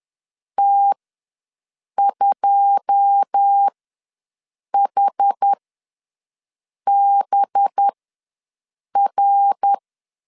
Morse code, heard as a keyed tone, about 10 wpm, spaced as for 6 wpm, 800 Hz. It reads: T2HBR
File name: LEZIONI / 10 SIGNS / 10D COMA ws 10wpm 4314